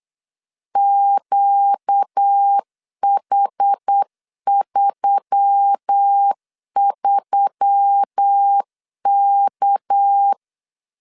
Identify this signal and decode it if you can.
Morse code, heard as a keyed tone, about 8 wpm, 800 Hz. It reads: QH33K
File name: CW / DET / 1 6 1500C S C 8wpm 5219